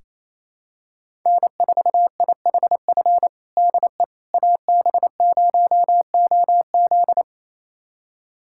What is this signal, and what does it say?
Morse code, heard as a keyed tone, about 28 wpm, 700 Hz. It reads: N4IHF DE AB0OZ
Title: USA Callsign DE Callsign - 5 Characters - 28wpm